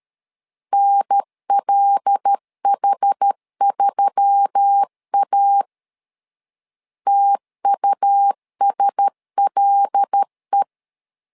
Morse code, heard as a keyed tone, about 13 wpm, 800 Hz. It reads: NLH3A TUSLE
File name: LEZIONI / 5 LBVC / 5 L 12wpm 1913